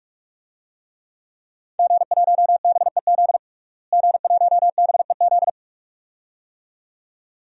Morse code, heard as a keyed tone, about 46 wpm, 700 Hz. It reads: G1BEZ G1BEZ
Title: DXCC-100 Call Signs - 5 Characters - Repeated 2x - 45wpm